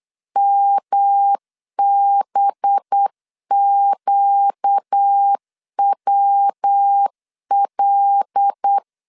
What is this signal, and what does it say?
Morse code, heard as a keyed tone, about 8 wpm, 800 Hz. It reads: MBQWL
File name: CW / DET / 1 6 1500C S B 8wpm 5142